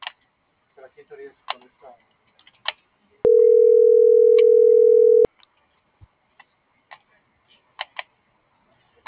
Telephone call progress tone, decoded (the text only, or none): ringback tone